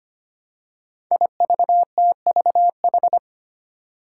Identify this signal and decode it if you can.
Morse code, heard as a keyed tone, about 25 wpm, 700 Hz. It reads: IVTVH
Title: Five Challenging Letter-Number Combinations 25wpm